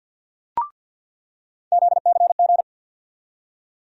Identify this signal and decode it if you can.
Morse code, heard as a keyed tone, about 51 wpm, 700 Hz. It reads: ZCG
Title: USA Call Sign Postfixes - 3 Letters 50wpm